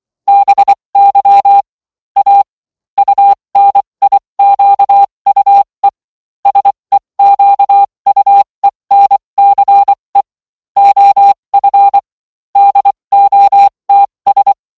Morse code, heard as a keyed tone, about 24 wpm, spaced as for 20 wpm, 800 Hz. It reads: BY A UNIQUE SEQUENCE OF DOTS